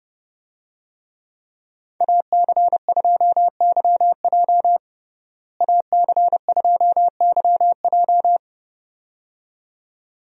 Morse code, heard as a keyed tone, about 32 wpm, 700 Hz. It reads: AC2YJ AC2YJ
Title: USA Call Signs - 5 Characters - Repeated 2x - 30wpm